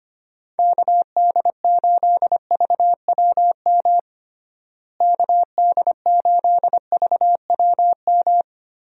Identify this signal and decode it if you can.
Morse code, heard as a keyed tone, about 25 wpm, 700 Hz. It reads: KD8VWM KD8VWM
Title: USA Call Signs - 6 Characters - Repeated 2x - 25wpm